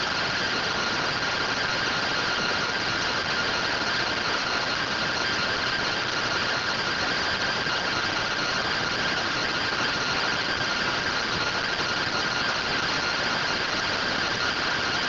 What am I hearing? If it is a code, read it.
Two-tone teleprinter signal, about 170 Hz shift.
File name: findings / MR AXIAL 33,99mm 1.gen-39 rtty rx